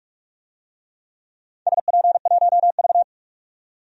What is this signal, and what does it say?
Morse code, heard as a keyed tone, about 45 wpm, 700 Hz. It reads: SP1V
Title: DXCC-100 Call Signs - 4 Characters - 45wpm